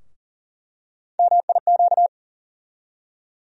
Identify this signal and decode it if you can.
Morse code, heard as a keyed tone, about 40 wpm, 700 Hz. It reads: MIQ
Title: USA Call Sign Postfixes - 3 Letters 40wpm